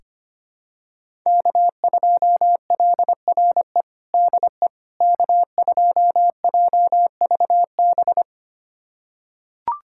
Morse code, heard as a keyed tone, about 25 wpm, 700 Hz. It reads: K2LRE DE K2JVB
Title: USA Callsign DE Callsign - 5 Characters - 25wpm